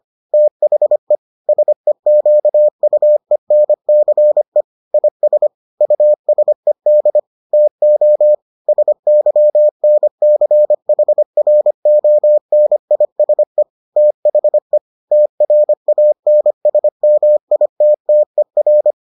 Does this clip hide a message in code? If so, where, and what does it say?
Yes: Morse code from 0.3 to 18.9 s, heard as a keyed tone, about 25 wpm, 600 Hz. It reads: THE SEQUENCE IS USED TO SYNCHRONISE THE TRANSMITTER